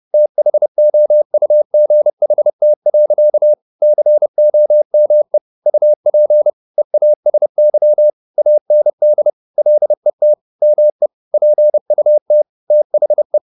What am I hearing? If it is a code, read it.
Morse code, heard as a keyed tone, about 30 wpm, 600 Hz. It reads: THOUGHT. COME UP EASY AND LET ME PUT THE
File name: the old man and the sea / the old man and the sea 1 30wpm